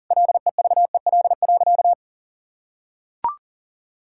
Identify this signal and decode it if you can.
Morse code, heard as a keyed tone, about 40 wpm, 700 Hz. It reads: LEVEL.
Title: Sentences from Top 1000 Words - Review Part 6 - 40wpm